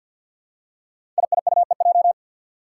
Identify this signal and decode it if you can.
Morse code, heard as a keyed tone, about 48 wpm, 700 Hz. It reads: IIUEJ
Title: Five Challenging Letter-Number Combinations 50wpm